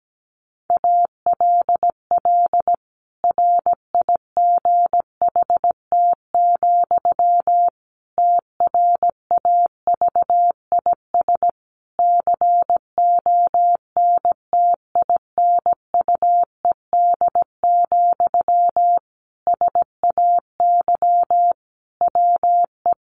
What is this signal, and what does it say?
Morse code, heard as a keyed tone, about 17 wpm, 700 Hz. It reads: ALL RIGHT, TRAVIS CONTINUED, SAY WE
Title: A Sound Of Thunder 17wpm